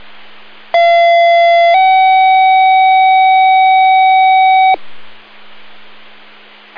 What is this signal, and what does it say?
Two-tone page, heard as a one-second tone, then a three-second tone, about 680 then 760 Hz, with noise. It is heard from 0.7 to 4.7 s.